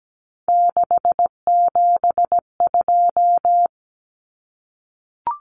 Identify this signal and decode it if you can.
Morse code, heard as a keyed tone, about 17 wpm, 700 Hz. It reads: 672
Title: Numbers - 3-Digit 17wpm